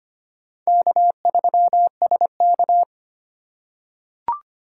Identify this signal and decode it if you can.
Morse code, heard as a keyed tone, about 25 wpm, 700 Hz. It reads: K3SK
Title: USA Call Signs - 4 Characters - 8x Character Spacing - 25wpm